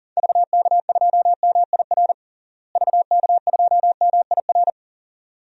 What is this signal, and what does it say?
Morse code, heard as a keyed tone, about 40 wpm, 700 Hz. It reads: VK2MIR VK2MIR
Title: DXCC-100 Call Signs - 6 Characters - Repeated 2x - 40wpm